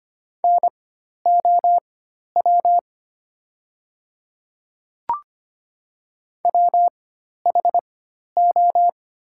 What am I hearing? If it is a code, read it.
Morse code, heard as a keyed tone, about 25 wpm, spaced as for 12 wpm, 700 Hz. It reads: NOW WHO
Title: 4x Character Spacing - Top 100 Words - 25wpm